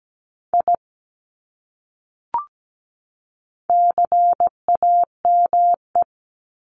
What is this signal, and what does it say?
Morse code, heard as a keyed tone, about 17 wpm, 700 Hz. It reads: I CAME
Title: Top 300 Words 17wpm